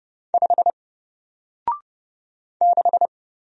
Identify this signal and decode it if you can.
Morse code, heard as a keyed tone, about 30 wpm, 700 Hz. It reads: 5 6
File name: Single-Letter-Number-30wpm